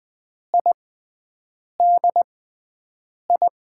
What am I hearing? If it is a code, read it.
Morse code, heard as a keyed tone, about 20 wpm, spaced as for 7 wpm, 700 Hz. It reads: IDI
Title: USA Call Signs - 6 Characters - 6x Character Spacing - 20wpm